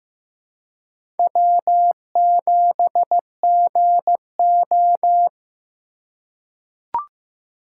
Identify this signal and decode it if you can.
Morse code, heard as a keyed tone, about 15 wpm, 700 Hz. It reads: W7GO
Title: USA Call Signs - 4 Characters - 2x Character Spacing - 15wpm